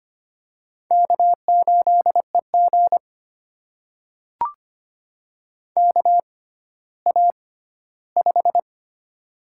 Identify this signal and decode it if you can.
Morse code, heard as a keyed tone, about 25 wpm, spaced as for 14 wpm, 700 Hz. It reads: K8EG KA5
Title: USA Call Signs - 4 Characters - 6x Character Spacing - 25wpm